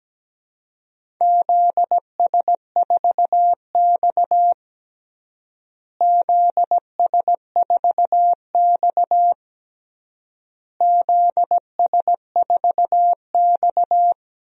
Morse code, heard as a keyed tone, about 17 wpm, 700 Hz. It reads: ZS4X ZS4X ZS4X
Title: DXCC-100 Call Signs - 4 Characters - Repeated 3x - 17wpm